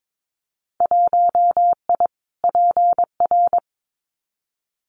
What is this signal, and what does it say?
Morse code, heard as a keyed tone, about 22 wpm, 700 Hz. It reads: 1I PR
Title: Winter Field Day Exchanges Encore 22wpm